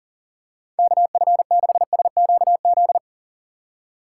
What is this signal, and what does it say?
Morse code, heard as a keyed tone, about 40 wpm, 700 Hz. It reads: KF6SQZ
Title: USA Call Signs - 3x Character Spacing - 40wpm